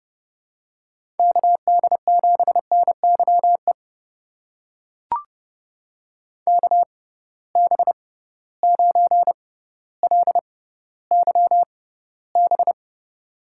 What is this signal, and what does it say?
Morse code, heard as a keyed tone, about 30 wpm, spaced as for 15 wpm, 700 Hz. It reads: KD7NYE KB9LYB
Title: USA Call Signs - 6 Characters - 6x Character Spacing - 30wpm